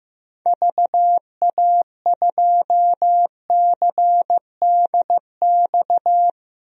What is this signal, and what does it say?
Morse code, heard as a keyed tone, about 16 wpm, 700 Hz. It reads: VA2CDX
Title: DXCC-100 Call Signs - 6 Characters - 15wpm